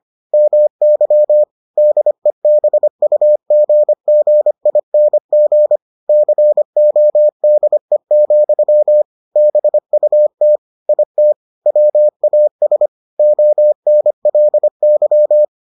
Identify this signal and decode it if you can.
Morse code, heard as a keyed tone, about 25 wpm, 600 Hz. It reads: MY DEBUGGING CODE, BUT IT WAS ONLY